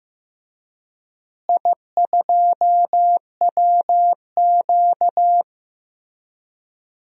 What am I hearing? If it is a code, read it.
Morse code, heard as a keyed tone, about 15 wpm, 700 Hz. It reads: I2WQ
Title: DXCC-100 Call Signs - 4 Characters - Repeated 3x - 15wpm